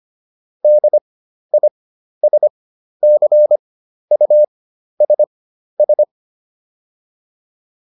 Morse code, heard as a keyed tone, about 25 wpm, spaced as for 12 wpm, 600 Hz. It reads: DISCUSS